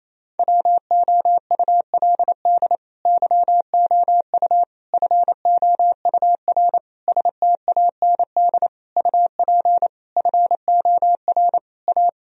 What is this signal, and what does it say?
Morse code, heard as a keyed tone, about 28 wpm, 700 Hz. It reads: WOULD YOU FOUR STAND UP FOR A
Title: Sentences from Top 1000 Words - Review Part 1 - 28wpm